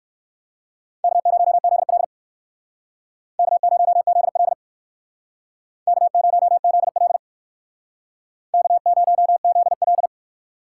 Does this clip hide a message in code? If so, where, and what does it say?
Yes: Morse code from 1.0 to 10.1 s, heard as a keyed tone, about 57 wpm, 700 Hz. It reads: K0ZL K0ZL K0ZL K0ZL